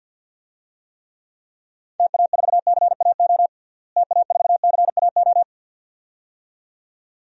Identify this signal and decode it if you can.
Morse code, heard as a keyed tone, about 49 wpm, 700 Hz. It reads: TA4CAO TA4CAO
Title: DXCC-100 Call Signs - 6 Characters - Repeated 2x - 50wpm